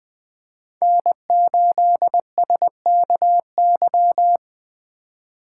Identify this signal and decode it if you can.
Morse code, heard as a keyed tone, about 24 wpm, 700 Hz. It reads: N8SKY
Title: Speed Racing - USA Call Signs 20wpm